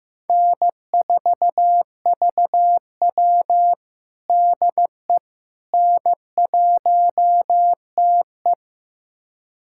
Morse code, heard as a keyed tone, about 16 wpm, 700 Hz. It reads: N4VW DE N1TE
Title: USA Callsign DE Callsign - 4 Characters - 15wpm